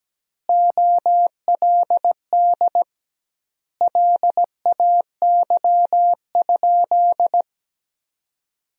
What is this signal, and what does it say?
Morse code, heard as a keyed tone, about 17 wpm, 700 Hz. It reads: OLD LAY?
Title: 2x Word-Spacing - Sentences from Top 400 Words - 17wpm